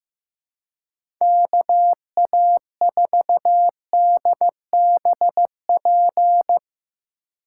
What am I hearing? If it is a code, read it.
Morse code, heard as a keyed tone, about 16 wpm, 700 Hz. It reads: KA4DBP
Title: USA Call Signs - 6 Characters - 2x Character Spacing - 15wpm